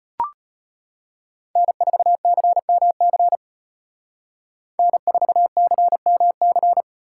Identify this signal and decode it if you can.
Morse code, heard as a keyed tone, about 30 wpm, 700 Hz. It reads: N4CMC N4CMC
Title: Speed Racing - USA 1x3 Call Signs 25wpm